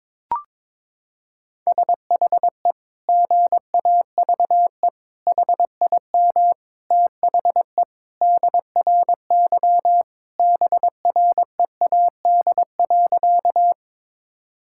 Morse code, heard as a keyed tone, about 22 wpm, 700 Hz. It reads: SHE GAVE HIM THE DRY BREAD.
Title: Sentences from Top 1600 Words 22wpm